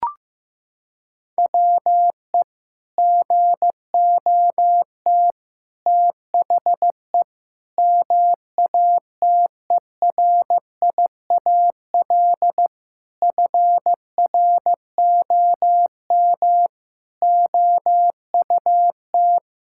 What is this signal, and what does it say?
Morse code, heard as a keyed tone, about 15 wpm, 700 Hz. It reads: WE GOT THE MATERIAL FROM OUT
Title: Sentences from Top 1000 Words - Review Part 5 - 15wpm